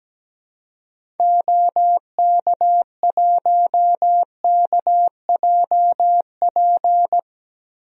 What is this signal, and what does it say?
Morse code, heard as a keyed tone, about 17 wpm, 700 Hz. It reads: OK1KJP
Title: DXCC-100 Call Signs - 6 Characters - 17wpm